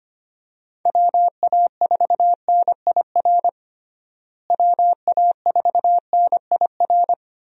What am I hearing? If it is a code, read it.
Morse code, heard as a keyed tone, about 25 wpm, 700 Hz. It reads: WA4NIR WA4NIR
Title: USA Call Signs - 6 Characters - Repeated 3x - 25wpm